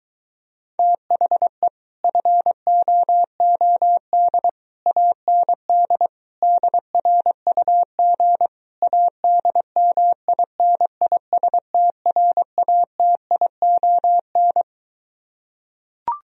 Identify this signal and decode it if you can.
Morse code, heard as a keyed tone, about 22 wpm, 700 Hz. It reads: THE FOOD AND DRUG ADMINISTRATION